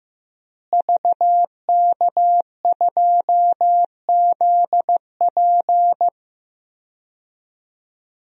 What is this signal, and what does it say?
Morse code, heard as a keyed tone, about 15 wpm, 700 Hz. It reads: VK2ZP
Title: DXCC-100 Call Signs - 5 Characters - Repeated 2x - 15wpm